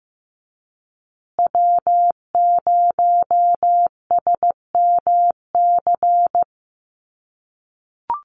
Morse code, heard as a keyed tone, about 15 wpm, 700 Hz. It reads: W0SMC
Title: USA Call Signs Repeated 2x 15wpm